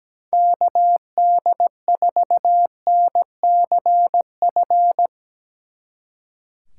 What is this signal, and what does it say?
Morse code, heard as a keyed tone, about 17 wpm, 700 Hz. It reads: KD4NCF
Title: Speed Racing - USA 2x3 Call Signs 17wpm